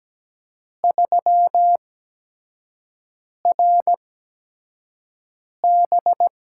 Morse code, heard as a keyed tone, about 17 wpm, spaced as for 5 wpm, 700 Hz. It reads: 3RB
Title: USA Call Signs - 4 Characters - 8x Character Spacing - 17wpm